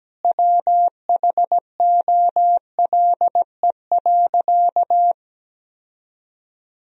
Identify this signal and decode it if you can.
Morse code, heard as a keyed tone, about 17 wpm, 700 Hz. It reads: WHOLE.
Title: Sentences from Top 1000 Words - Review Part 1 - 17wpm